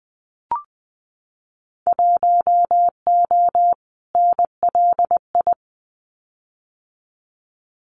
Morse code, heard as a keyed tone, about 19 wpm, 700 Hz. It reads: 1O NLI
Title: Winter Field Day Exchanges 20wpm